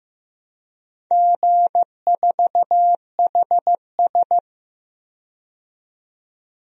Morse code, heard as a keyed tone, about 15 wpm, 700 Hz. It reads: G4HS
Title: DXCC-100 Call Signs - 4 Characters - 15wpm